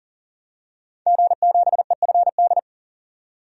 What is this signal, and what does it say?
Morse code, heard as a keyed tone, about 40 wpm, 700 Hz. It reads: G7EFD
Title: DXCC-100 Call Signs - 5 Characters - 40wpm